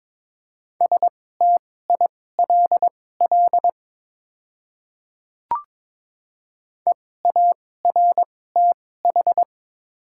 Morse code, heard as a keyed tone, about 22 wpm, spaced as for 16 wpm, 700 Hz. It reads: STILL EARTH
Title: 2x Character Spacing - Top 500 Words - 22wpm